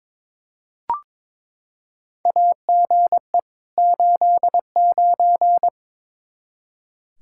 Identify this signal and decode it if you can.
Morse code, heard as a keyed tone, about 22 wpm, 700 Hz. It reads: AGE 89
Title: Practice QSO - Exchange 3 - More Info - 22wpm